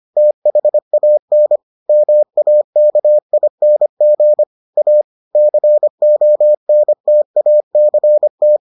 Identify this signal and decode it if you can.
Morse code, heard as a keyed tone, about 25 wpm, 600 Hz. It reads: THAN MAKING A CONTACT